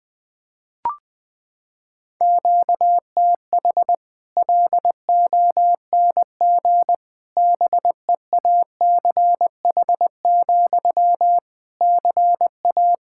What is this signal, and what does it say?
Morse code, heard as a keyed tone, about 20 wpm, 700 Hz. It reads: QTH LONG BEACH, CA
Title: Practice QSO - Exchange 1 - Beginning - 20wpm